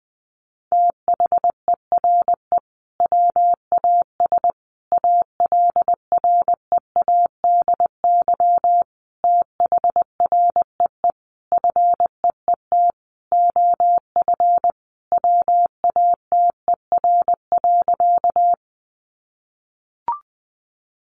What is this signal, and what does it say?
Morse code, heard as a keyed tone, about 20 wpm, 700 Hz. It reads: THERE WAS ALREADY THREE FEET OF WATER.
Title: Sentences from Top 700 Words 20wpm